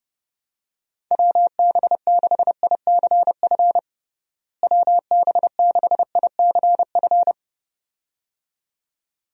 Morse code, heard as a keyed tone, about 30 wpm, 700 Hz. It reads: WB6ICF WB6ICF
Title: USA Call Signs - 6 Characters - Repeated 2x - 30wpm